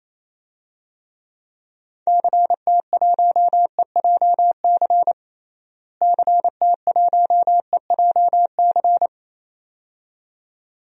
Morse code, heard as a keyed tone, about 28 wpm, 700 Hz. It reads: CT1EJC CT1EJC
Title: DXCC-100 Call Signs - 6 Characters - Repeated 2x - 28wpm